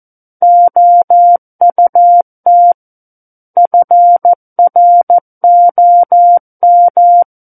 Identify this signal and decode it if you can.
Morse code, heard as a keyed tone, about 14 wpm, 700 Hz. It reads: OUT FROM
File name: Book One / 05-Chapter 5 - 14 WPM